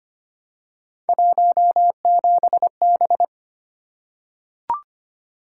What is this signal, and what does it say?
Morse code, heard as a keyed tone, about 25 wpm, 700 Hz. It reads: 17B
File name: Three Letter-Number Combinations - 25wpm